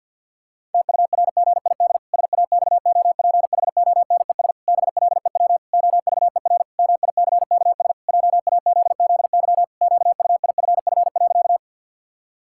Morse code, heard as a keyed tone, about 53 wpm, 700 Hz. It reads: TURGID SAXOPHONES BLEW OVER MICKS JAZZY QUIFF.
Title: Pangram Challenge v2 - 50wpm